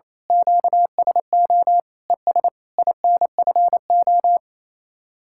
Morse code, heard as a keyed tone, about 27 wpm, 700 Hz. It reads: QSO ES INFO
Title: Speed-Racing Common QSO Phrases - 28wpm